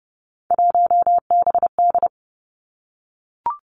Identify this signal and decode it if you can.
Morse code, heard as a keyed tone, about 30 wpm, 700 Hz. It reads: 1BD
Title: Three Challenging Letter-Number Combinations 30wpm